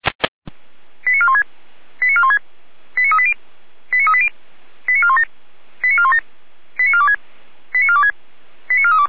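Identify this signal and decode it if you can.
Five-tone selective call (ZVEI), heard as quick runs of five tones, tones of about 70 ms. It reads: 89416 89416 89380 89380 89417 89417 89426 89426 89432
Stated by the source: Received on SDR; ZVEI five-tone call 89350, channel 455.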